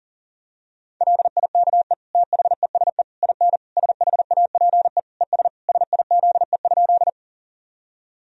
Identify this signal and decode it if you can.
Morse code, heard as a keyed tone, about 37 wpm, 700 Hz. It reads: LIKE THESE IN SHAPE ES SIZE?
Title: Sentences from Top 500 Words and Abbreviations 40wpm